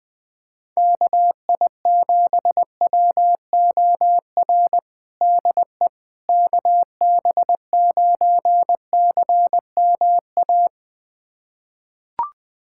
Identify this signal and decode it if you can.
Morse code, heard as a keyed tone, about 20 wpm, 700 Hz. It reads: KI7WOR DE KB9CMA
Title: USA Callsign DE Callsign - 6 Characters - 20wpm